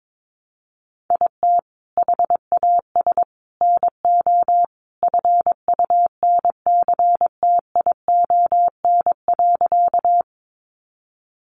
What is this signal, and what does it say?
Morse code, heard as a keyed tone, about 22 wpm, 700 Hz. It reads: IT HAS NO FUNCTION.
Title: Sentences from Top 1100 Words 22wpm